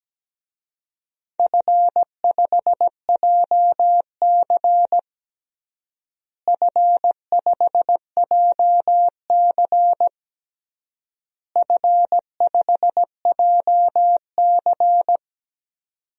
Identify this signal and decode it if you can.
Morse code, heard as a keyed tone, about 17 wpm, 700 Hz. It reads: F5JC F5JC F5JC
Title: DXCC-100 Call Signs - 4 Characters - Repeated 3x - 17wpm